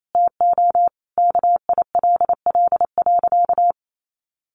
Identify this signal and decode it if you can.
Morse code, heard as a keyed tone, about 28 wpm, 700 Hz. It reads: TO KILL.
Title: Sentences from Top 1400 Words 28wpm